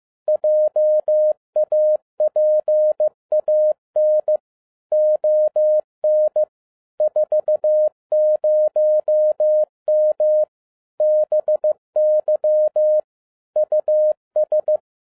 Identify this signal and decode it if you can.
Morse code, heard as a keyed tone, about 15 wpm, 610 Hz. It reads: JAPAN ON 40M BY US
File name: 2011-07-mp3 / n47 JOHN K3TN 15wpm